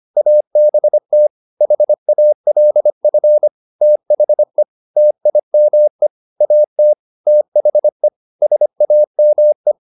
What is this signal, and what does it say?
Morse code, heard as a keyed tone, about 25 wpm, 600 Hz. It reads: ABT HALF THE TIME AT THE SAME